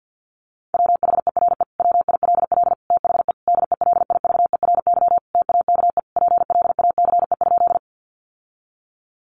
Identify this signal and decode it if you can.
Morse code, heard as a keyed tone, about 48 wpm, 700 Hz. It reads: WHERE WILL THE DELIVERY TAKE PLACE?
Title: Sentences from Top 1500 Words 50wpm